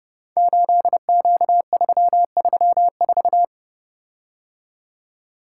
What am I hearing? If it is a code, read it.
Morse code, heard as a keyed tone, about 30 wpm, 700 Hz. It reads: 8Q334
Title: Code Groups - 5 Letter:Numbers - 30wpm